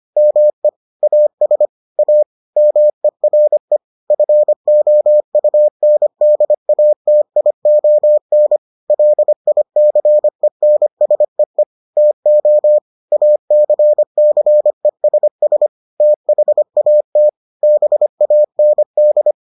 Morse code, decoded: ME AS A MERE FOUNDATION LICENSEE TO ACCESS THAT BAND

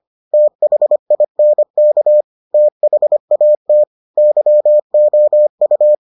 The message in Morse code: THINK THAT YOU

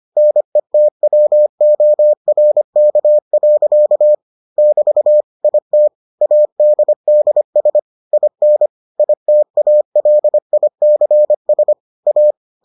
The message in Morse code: NETWORK. <BT> IT ADDS IN ITALICS A

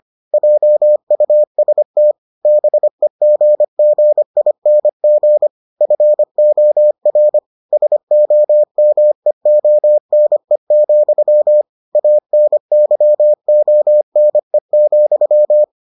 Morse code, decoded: JUST BEGGING FOR SOMEONE, ANYONE,

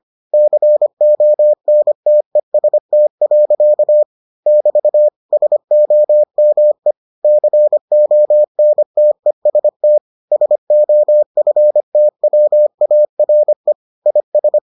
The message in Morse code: CONTEST. <BT> SOME CONTEST SOFTWARE IS

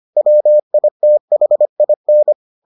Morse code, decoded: WITHIN

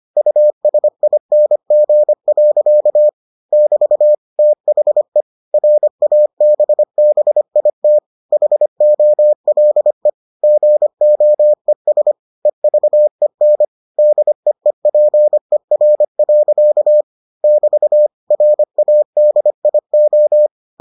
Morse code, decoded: USING. <BT> THE RABBIT HOLE GOES EVEN DEEPER. <BT> RADIO

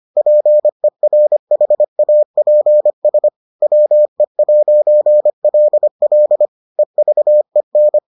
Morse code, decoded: PERHAPS WE'LL EVEN

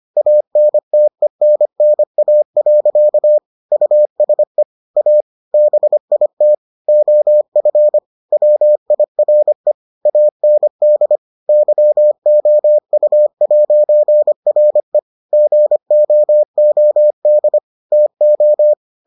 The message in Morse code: ANTENNA. USE A BIT OF WIRE AND YOU'RE GOOD TO

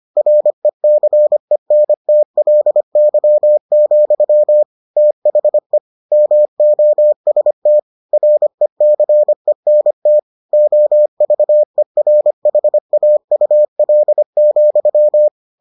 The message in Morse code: RECENTLY, THE MOST RECENT OVERHAUL,

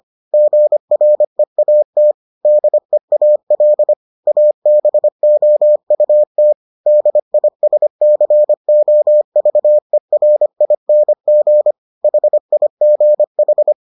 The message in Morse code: GREAT DEAL ABOUT DISCOVERING HIGH

